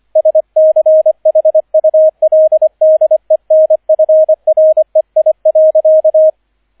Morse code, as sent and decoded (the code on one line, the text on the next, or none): ... -.-. .... ..- .-.. -.. -. ..-. .-. .. .-.-.-
SCHULDENFREI.